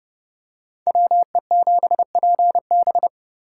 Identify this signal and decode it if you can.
Morse code, heard as a keyed tone, about 29 wpm, 700 Hz. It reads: WE7PB